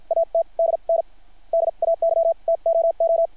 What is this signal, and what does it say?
Morse code, heard as a keyed tone, about 50 wpm, spaced as for 42 wpm, 660 Hz. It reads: AT DN DAY TOO
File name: archived / CRSnet